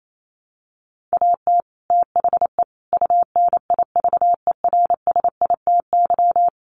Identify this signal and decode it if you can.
Morse code, heard as a keyed tone, about 28 wpm, 700 Hz. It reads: AT THE UNIVERSITY
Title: Sets of 4 Words 28wpm